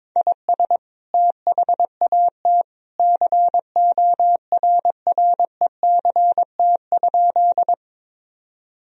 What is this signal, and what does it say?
Morse code, heard as a keyed tone, about 22 wpm, 700 Hz. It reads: IS THAT CORRECT?
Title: Sentences from Top 1000 Words - Review Part 2 - 22wpm